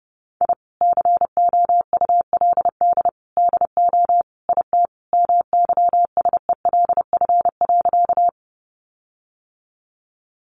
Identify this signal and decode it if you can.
Morse code, heard as a keyed tone, about 30 wpm, 700 Hz. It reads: I COULD DO IT MYSELF.
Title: Sentences from Top 1300 Words 30wpm